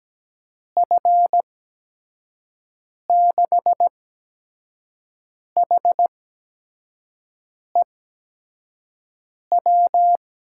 Morse code, heard as a keyed tone, about 17 wpm, spaced as for 5 wpm, 700 Hz. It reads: F6HEW
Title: USA Call Signs - 6 Characters - 8x Character Spacing - 17wpm